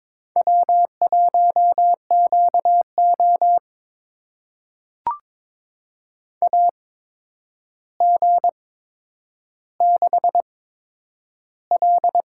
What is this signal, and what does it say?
Morse code, heard as a keyed tone, about 22 wpm, spaced as for 10 wpm, 700 Hz. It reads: W1QO AG6L
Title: USA Call Signs - 4 Characters - 8x Character Spacing - 22wpm